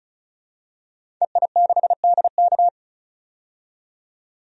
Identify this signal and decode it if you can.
Morse code, heard as a keyed tone, about 36 wpm, 700 Hz. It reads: EI6DK
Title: DXCC-100 Call Signs - 5 Characters - 35wpm